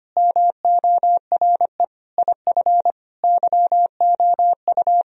Morse code, heard as a keyed tone, about 25 wpm, 700 Hz. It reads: MORE IF YOU
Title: Sentences from Top 1000 Words - Review Part 1 - 25wpm